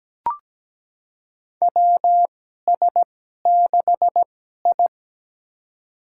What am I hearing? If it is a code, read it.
Morse code, heard as a keyed tone, about 17 wpm, spaced as for 12 wpm, 700 Hz. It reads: WS6I
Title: USA Call Signs - 4 Characters - 2x Character Spacing - 17wpm